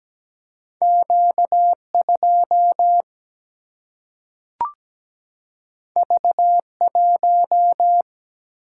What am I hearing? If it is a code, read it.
Morse code, heard as a keyed tone, about 17 wpm, 700 Hz. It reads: Q2 V1
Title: Single-Letter Single-Number 17wpm